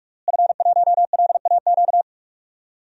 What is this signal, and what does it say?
Morse code, heard as a keyed tone, about 42 wpm, 700 Hz. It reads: F1LAQ
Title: DXCC-100 Call Signs - 5 Characters - Repeated 2x - 45wpm